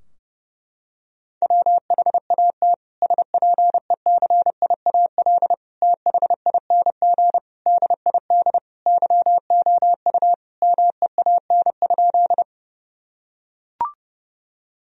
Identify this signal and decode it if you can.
Morse code, heard as a keyed tone, about 30 wpm, 700 Hz. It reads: WHAT SPECIAL THING DID YOU MEAN?
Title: Sentences from Top 1000 Words - Review Part 3 - 30wpm